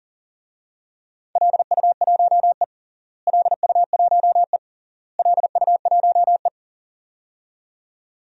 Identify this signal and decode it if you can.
Morse code, heard as a keyed tone, about 40 wpm, 700 Hz. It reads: LU1E LU1E LU1E
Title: DXCC-100 Call Signs - 4 Characters - Repeated 3x - 40wpm